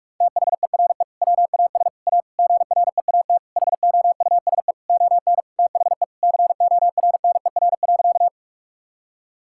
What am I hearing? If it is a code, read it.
Morse code, heard as a keyed tone, about 46 wpm, 700 Hz. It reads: THERE WAS A GREAT HOUSE ON THE CORNER.
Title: Sentences from Top 1600 Words 45wpm